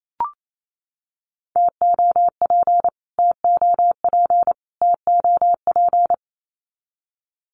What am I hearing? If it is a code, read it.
Morse code, heard as a keyed tone, about 28 wpm, 700 Hz. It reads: TOP TOP TOP
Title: Top 500 Words Repeated 28wpm